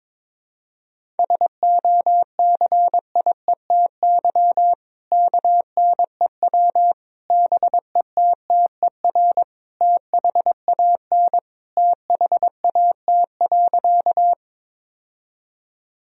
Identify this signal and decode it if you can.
Morse code, heard as a keyed tone, about 21 wpm, 700 Hz. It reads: SOCIETY KNEW BETTER THAN THAT.
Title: Sentences from Top 1000 Words - Review Part 4 - 22wpm